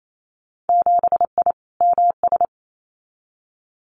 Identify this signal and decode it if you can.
Morse code, heard as a keyed tone, about 28 wpm, 700 Hz. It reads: 7I MS
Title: Winter Field Day Exchanges Encore 28wpm